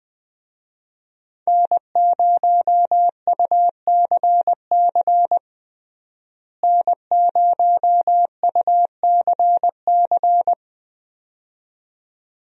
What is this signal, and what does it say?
Morse code, heard as a keyed tone, about 21 wpm, 700 Hz. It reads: N0UCC N0UCC
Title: USA Call Signs - 5 Characters - Repeated 2x - 20wpm